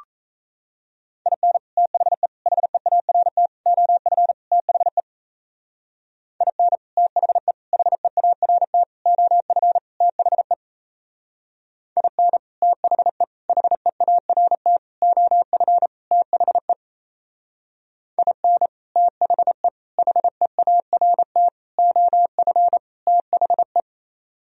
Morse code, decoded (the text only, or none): IN THE HEART OF THE IN THE HEART OF THE IN THE HEART OF THE IN THE HEART OF THE